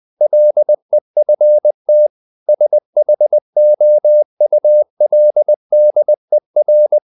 Morse code, heard as a keyed tone, about 20 wpm, 600 Hz. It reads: LEFT SHOULDER
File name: the old man and the sea / the old man and the sea part 1 20wpm